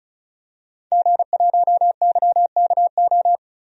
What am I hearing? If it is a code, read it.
Morse code, heard as a keyed tone, about 36 wpm, 700 Hz. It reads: G1YKO
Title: DXCC-100 Call Signs - 5 Characters - Repeated 3x - 35wpm